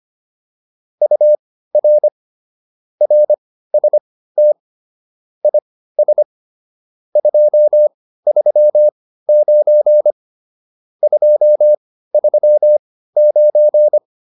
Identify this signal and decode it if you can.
Morse code, heard as a keyed tone, about 25 wpm, spaced as for 15 wpm, 600 Hz. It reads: UR RST IS 239 239
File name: QSO308 15